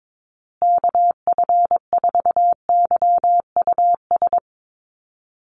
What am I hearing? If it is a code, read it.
Morse code, heard as a keyed tone, about 22 wpm, 700 Hz. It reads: KF4YUS
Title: USA Call Signs Repeated 3x 22wpm